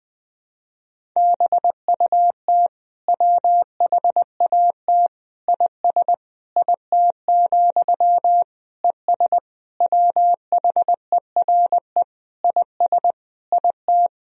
Morse code, heard as a keyed tone, about 20 wpm, 700 Hz. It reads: BUT WHAT IS IT, ES WHERE IS IT
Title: Sentences from Top 300 Words and Abbreviations 20wpm